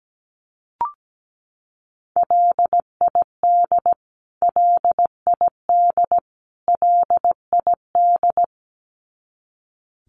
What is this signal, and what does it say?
Morse code, heard as a keyed tone, about 17 wpm, 700 Hz. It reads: LID LID LID
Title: Abbreviations Repeated 3x - 17wpm